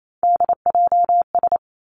Morse code, heard as a keyed tone, about 28 wpm, 700 Hz. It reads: DJS
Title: Three Challenging Letter-Number Combinations 28wpm